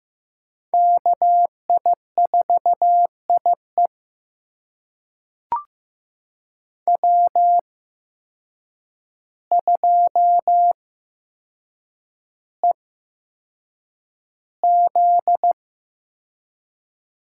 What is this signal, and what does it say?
Morse code, heard as a keyed tone, about 15 wpm, spaced as for 7 wpm, 700 Hz. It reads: KI4IE W2EZ
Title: USA Call Signs - 5 Characters - 8x Character Spacing - 15wpm